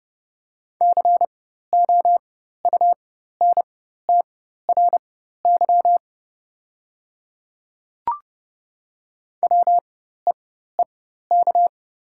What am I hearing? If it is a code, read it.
Morse code, heard as a keyed tone, about 30 wpm, spaced as for 14 wpm, 700 Hz. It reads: COUNTRY WEEK
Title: 4x Character Spacing - Top 500 Words - 30wpm